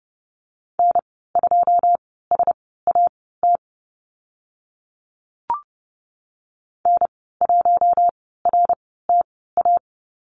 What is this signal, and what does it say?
Morse code, heard as a keyed tone, about 30 wpm, spaced as for 17 wpm, 700 Hz. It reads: N2SAT N1RTA
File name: Lesson 15g - Number 5 - Call Signs 30wpm